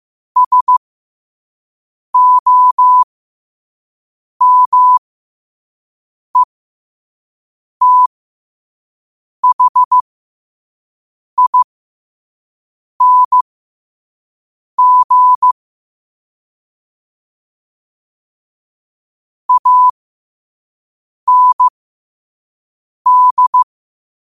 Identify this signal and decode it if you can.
Morse code, heard as a keyed tone, about 15 wpm, spaced as for 5 wpm, 1000 Hz. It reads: SOMETHING AND